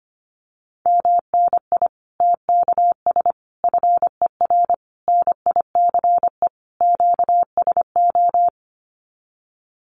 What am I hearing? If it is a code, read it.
Morse code, heard as a keyed tone, about 25 wpm, 700 Hz. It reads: MNI TKS FER NICE QSO
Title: Common QSO Phrases 25wpm